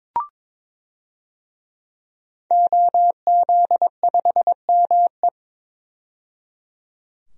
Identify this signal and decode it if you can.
Morse code, heard as a keyed tone, about 22 wpm, 700 Hz. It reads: OZ5ME
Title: DXCC-100 Call Signs - 5 Characters - 22wpm